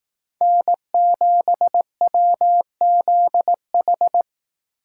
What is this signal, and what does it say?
Morse code, heard as a keyed tone, about 18 wpm, 700 Hz. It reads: N7WZH
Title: Speed Racing - USA 1x3 Call Signs 15wpm